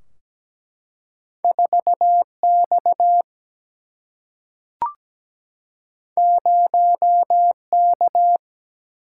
Morse code, heard as a keyed tone, about 17 wpm, 700 Hz. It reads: 4 X 0 K
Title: Single-Letter Single-Number 17wpm